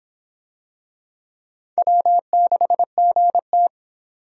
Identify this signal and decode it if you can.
Morse code, heard as a keyed tone, about 26 wpm, 700 Hz. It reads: W6GT